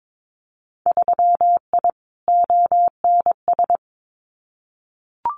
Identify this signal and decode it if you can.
Morse code, heard as a keyed tone, about 22 wpm, 700 Hz. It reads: 3I ONS
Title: Winter Field Day Exchanges Encore 22wpm